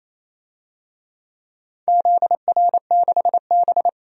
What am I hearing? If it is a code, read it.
Morse code, heard as a keyed tone, about 27 wpm, 700 Hz. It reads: ZR6B
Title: DXCC-100 Call Signs - 4 Characters - 28wpm